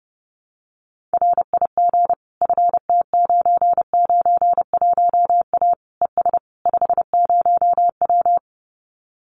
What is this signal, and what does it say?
Morse code, heard as a keyed tone, about 30 wpm, 700 Hz. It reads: RIG FT991A ES 50W
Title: Common QSO Phrases Encore 30wpm